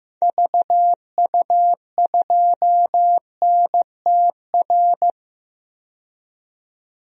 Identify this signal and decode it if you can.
Morse code, heard as a keyed tone, about 15 wpm, 700 Hz. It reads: VU2NTR
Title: DXCC-100 Call Signs - 6 Characters - 15wpm